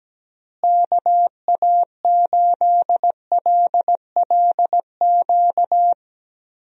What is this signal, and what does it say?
Morse code, heard as a keyed tone, about 17 wpm, 700 Hz. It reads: KA8LLQ
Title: USA Call Signs - 6 Characters - Repeated 3x - 17wpm